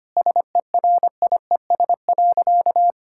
Morse code, heard as a keyed tone, about 25 wpm, 700 Hz. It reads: SERIES.
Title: Sentences from Top 1000 Words - Review Part 5 - 25wpm